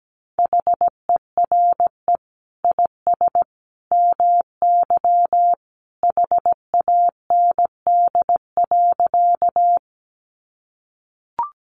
Morse code, heard as a keyed tone, about 17 wpm, 700 Hz. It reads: HERE IS MY HAND.
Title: Sentences from Top 200 Words 17wpm